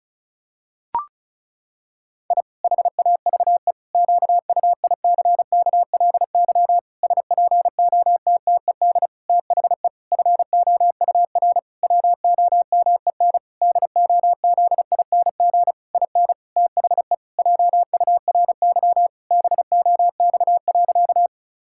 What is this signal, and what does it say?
Morse code, heard as a keyed tone, about 34 wpm, 700 Hz. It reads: I HAVE QUICKLY SPOTTED THE FOUR WOMEN DOZING IN THE JURY BOX.
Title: Pangram Challenge v2 - 35wpm